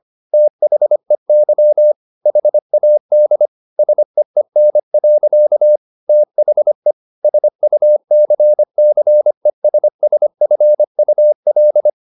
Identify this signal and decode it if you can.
Morse code, heard as a keyed tone, about 25 wpm, 600 Hz. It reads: THEY HAD SEEN. THE SUCCESSFUL
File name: the old man and the sea part 1 25wpm